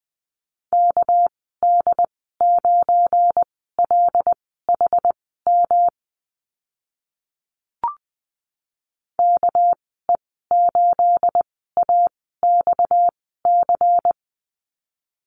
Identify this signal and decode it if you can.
Morse code, heard as a keyed tone, about 20 wpm, spaced as for 15 wpm, 700 Hz. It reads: KD9LHM KE8AXC
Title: 245 Lesson 39g - Letter z - Call Signs 20wpm